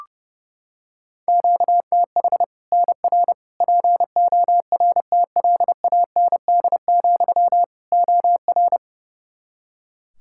Morse code, heard as a keyed tone, about 30 wpm, 700 Hz. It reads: QTH NR PORTLAND, OR
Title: Practice QSO - Exchange 1 - Beginning - 30wpm